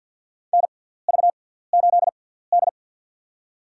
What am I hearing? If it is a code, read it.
Morse code, heard as a keyed tone, about 53 wpm, spaced as for 17 wpm, 700 Hz. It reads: NV8D
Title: USA Call Signs - 4 Characters - 6x Character Spacing - 50wpm